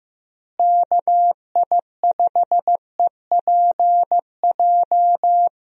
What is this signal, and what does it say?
Morse code, heard as a keyed tone, about 15 wpm, 700 Hz. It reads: KI5EPJ
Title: USA Call Signs - 6 Characters - Repeated 2x - 15wpm